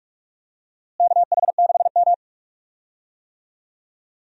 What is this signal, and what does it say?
Morse code, heard as a keyed tone, about 46 wpm, 700 Hz. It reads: KH6M
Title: USA Call Signs - 4 Characters - Repeated 2x - 45wpm